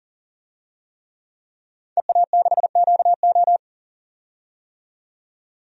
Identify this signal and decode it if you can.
Morse code, heard as a keyed tone, about 37 wpm, 700 Hz. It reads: EA6QO
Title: DXCC-100 Call Signs - 5 Characters - 40wpm